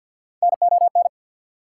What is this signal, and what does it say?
Morse code, heard as a keyed tone, about 52 wpm, 690 Hz. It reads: NON
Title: Prefixes and Suffixes 50wpm